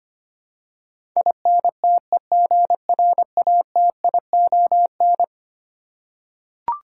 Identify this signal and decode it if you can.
Morse code, heard as a keyed tone, about 25 wpm, 700 Hz. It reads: INTEGRATION
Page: morse-code-ninja-practice-25wpm